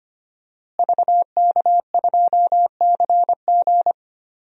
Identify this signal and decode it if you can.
Morse code, heard as a keyed tone, about 25 wpm, 700 Hz. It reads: VK2CG
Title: DXCC-100 Call Signs - 5 Characters - Repeated 2x - 25wpm